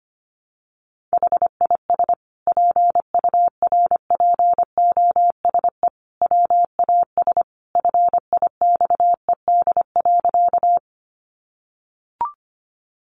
Morse code, heard as a keyed tone, about 25 wpm, 700 Hz. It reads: HIS PURPOSE WAS FIXED.
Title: Sentences from Top 1100 Words 25wpm